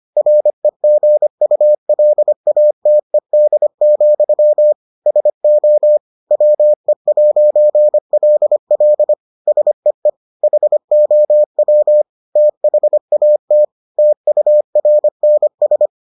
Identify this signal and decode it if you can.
Morse code, heard as a keyed tone, about 25 wpm, 600 Hz. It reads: REGULATED, SO WE'LL SEE HOW THAT TURNS